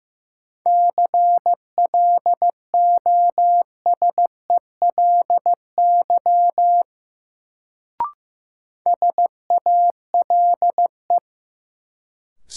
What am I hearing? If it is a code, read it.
Morse code, heard as a keyed tone, about 15 wpm, 700 Hz. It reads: CLOSELY SALE
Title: Sentences from Top 1400 Words - New Word Drill Alt - 15wpm